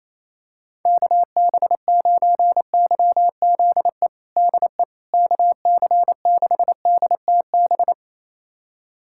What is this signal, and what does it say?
Morse code, heard as a keyed tone, about 28 wpm, 700 Hz. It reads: KB9YZE DE KC6DTB
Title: USA Callsign DE Callsign - 6 Characters - 28wpm